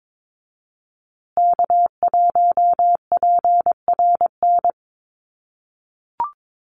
Morse code, heard as a keyed tone, about 22 wpm, 700 Hz. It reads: K1PRN
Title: USA Call Signs Repeated 2x 22wpm